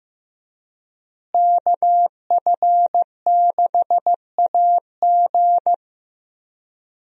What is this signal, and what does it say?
Morse code, heard as a keyed tone, about 16 wpm, 700 Hz. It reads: KF6AG
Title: USA Call Signs - 5 Characters - Repeated 2x - 15wpm